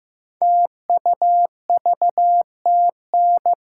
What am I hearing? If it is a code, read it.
Morse code, heard as a keyed tone, about 15 wpm, 700 Hz. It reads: TUVTN
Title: Code Groups - 5 Letters - 15wpm